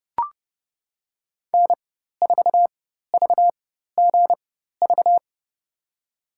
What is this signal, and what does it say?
Morse code, heard as a keyed tone, about 30 wpm, spaced as for 14 wpm, 700 Hz. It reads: N4VGV
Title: USA Call Signs - 4x Character Spacing - 30wpm